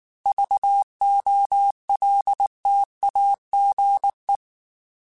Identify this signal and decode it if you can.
Morse code, heard as a keyed tone, about 19 wpm, 800 Hz. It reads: VOLTAGE
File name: Technical Subjects / Technical Subjects-mp3s / XC @18